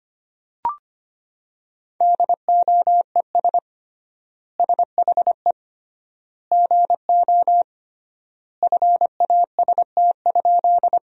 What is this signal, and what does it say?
Morse code, heard as a keyed tone, about 25 wpm, 700 Hz. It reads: DOES SHE GO FAST?
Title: 3x Word-Spacing - Sentences from Top 400 Words - 25wpm